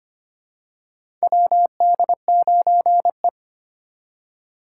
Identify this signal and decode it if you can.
Morse code, heard as a keyed tone, about 25 wpm, 700 Hz. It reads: WD9E
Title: USA Call Signs - 2x Character Spacing - 25wpm